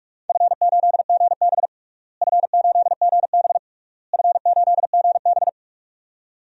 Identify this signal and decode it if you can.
Morse code, heard as a keyed tone, about 46 wpm, 700 Hz. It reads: F8GB F8GB F8GB
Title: DXCC-100 Call Signs - 4 Characters - Repeated 3x - 45wpm